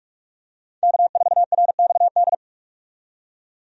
Morse code, heard as a keyed tone, about 43 wpm, 700 Hz. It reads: K4RXD
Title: USA Call Signs - 5 Characters - 45wpm